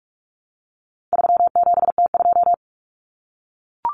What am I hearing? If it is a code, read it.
Morse code, heard as a keyed tone, about 44 wpm, 700 Hz. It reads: 37T2
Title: Common Cut Numbers - 4-Digit 45wpm